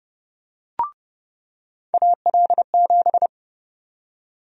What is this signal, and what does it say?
Morse code, heard as a keyed tone, about 32 wpm, 700 Hz. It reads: AL7
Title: USA Call Sign Prefixes - 30wpm